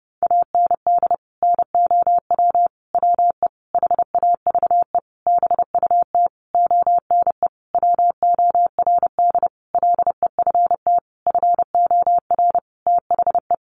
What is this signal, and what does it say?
Morse code, heard as a keyed tone, about 30 wpm, 700 Hz. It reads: AND NOW WE HAVE BUT ONE WORD LEFT FOR THE
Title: Sentences from Top 1500 Words 30wpm